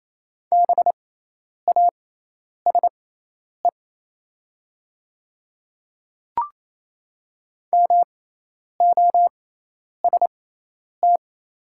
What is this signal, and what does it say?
Morse code, heard as a keyed tone, about 27 wpm, spaced as for 10 wpm, 700 Hz. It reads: BASE MOST
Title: 6x Character Spacing - Top 300 Words - 28wpm